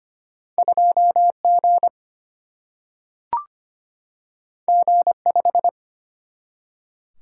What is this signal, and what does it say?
Morse code, heard as a keyed tone, about 25 wpm, 700 Hz. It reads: 2G G5
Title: Single-Letter Single-Number 25wpm